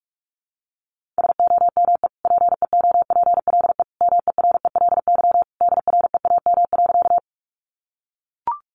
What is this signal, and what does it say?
Morse code, heard as a keyed tone, about 44 wpm, 700 Hz. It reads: SOME PEOPLE MERELY DREAM.
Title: Sentences from Top 1500 Words 45wpm